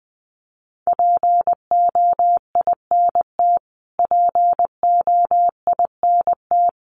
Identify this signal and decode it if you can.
Morse code, heard as a keyed tone, about 20 wpm, 700 Hz. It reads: POINT POINT
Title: Top 200 Words Repeated 20wpm